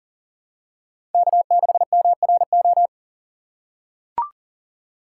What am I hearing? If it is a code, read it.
Morse code, heard as a keyed tone, about 39 wpm, 700 Hz. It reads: K6MRO
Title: USA Call Signs - 5 Characters - 3x Character Spacing - 40wpm